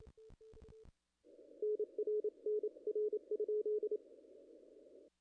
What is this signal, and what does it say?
Morse code, heard as a keyed tone, about 28 wpm, 440 Hz. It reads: NR NR?
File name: contests / ss-cw-k5tr-op-n5rz -0000